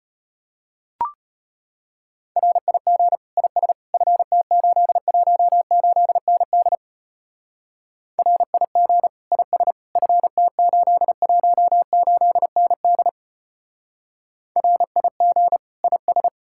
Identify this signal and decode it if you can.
Morse code, heard as a keyed tone, about 32 wpm, 700 Hz. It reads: RIG IS FT818ND RIG IS FT818ND RIG IS